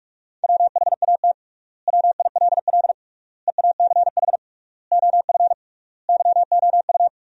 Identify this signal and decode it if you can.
Morse code, heard as a keyed tone, about 46 wpm, 700 Hz. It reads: WHAT WILL EACH OF YOU
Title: 3x Word-Spacing - Sentences from Top 200 Words - 45wpm